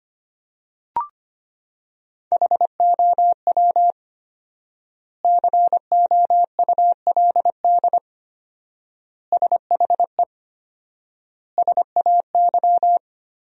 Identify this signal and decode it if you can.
Morse code, heard as a keyed tone, about 25 wpm, 700 Hz. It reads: HOW COULD SHE SAY
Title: 4x Word-Spacing - Sentences from Top 200 Words - 25wpm